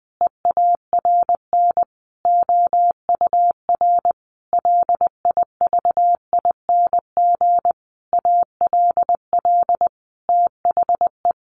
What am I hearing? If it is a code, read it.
Morse code, heard as a keyed tone, about 20 wpm, 700 Hz. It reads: EARN OUR LIVING ALL THE
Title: Sentences from Top 1500 Words 20wpm